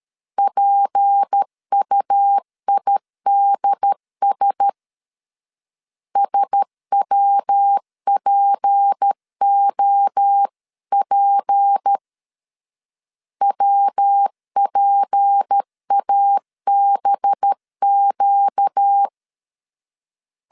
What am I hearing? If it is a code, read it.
Morse code, heard as a keyed tone, about 13 wpm, 800 Hz. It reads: PUIDS SWPOP WPABQ
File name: LEZIONI / 6 PKX0 / 6 P 12wpm 2033